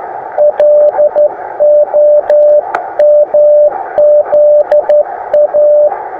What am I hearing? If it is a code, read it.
Morse code, heard as a keyed tone, about 13 wpm, 590 Hz. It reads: LOMZA
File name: attachments / sp4ipl 20m